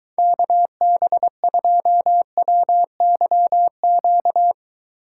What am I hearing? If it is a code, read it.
Morse code, heard as a keyed tone, about 22 wpm, 700 Hz. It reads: KB2WYQ